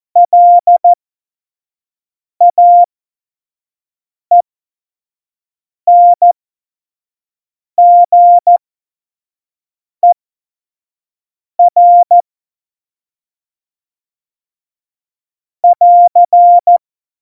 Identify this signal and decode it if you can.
Morse code, heard as a keyed tone, about 14 wpm, spaced as for 5 wpm, 700 Hz. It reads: LAENGER <AR>
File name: Morsekurs / Witze / Witz 28. November 2017, 25 F70